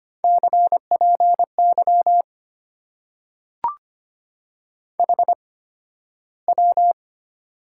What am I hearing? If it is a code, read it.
Morse code, heard as a keyed tone, about 25 wpm, spaced as for 13 wpm, 700 Hz. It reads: CPY HW
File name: Top 90 QSO Elements - 8x Character Spacing - 25wpm